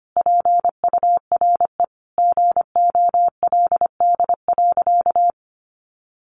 Morse code, heard as a keyed tone, about 25 wpm, 700 Hz. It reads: PURE GOLD.
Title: Sentences from Top 1500 Words 25wpm